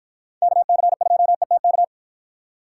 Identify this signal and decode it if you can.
Morse code, heard as a keyed tone, about 53 wpm, 700 Hz. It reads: KC2ETK